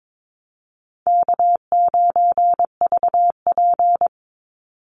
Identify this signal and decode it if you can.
Morse code, heard as a keyed tone, about 22 wpm, 700 Hz. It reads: K9VP
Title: Call Signs - 22wpm